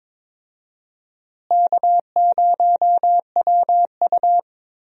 Morse code, heard as a keyed tone, about 22 wpm, 700 Hz. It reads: K0WU
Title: USA Call Signs - 4 Characters - Repeated 2x - 22wpm